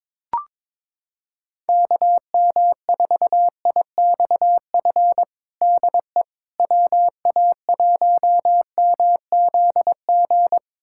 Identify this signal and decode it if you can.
Morse code, heard as a keyed tone, about 22 wpm, 700 Hz. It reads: KM4IXF DE WA1MZG
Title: USA Callsign DE Callsign 22wpm